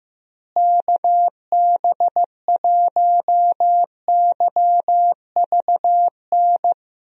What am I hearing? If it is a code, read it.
Morse code, heard as a keyed tone, about 15 wpm, 700 Hz. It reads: KB1YVN
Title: USA Call Signs - 6 Characters - Repeated 2x - 15wpm